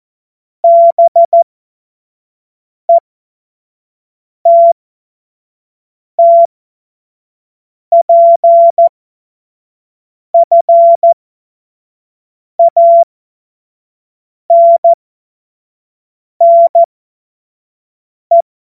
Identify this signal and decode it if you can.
Morse code, heard as a keyed tone, about 14 wpm, spaced as for 5 wpm, 680 Hz. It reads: BETTPFANNE
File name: Morsekurs / Witze / Witz 06 Februar 2024 25 F70